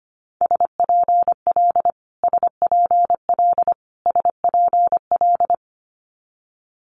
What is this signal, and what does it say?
Morse code, heard as a keyed tone, about 25 wpm, 700 Hz. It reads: SPL SPL SPL
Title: Multi-letter Phonemes Repeated 3x 25wpm